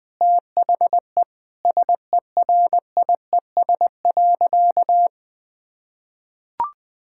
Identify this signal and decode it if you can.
Morse code, heard as a keyed tone, about 20 wpm, 700 Hz. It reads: THE SERIES.
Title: Sentences from Top 1000 Words - Review Part 5 - 20wpm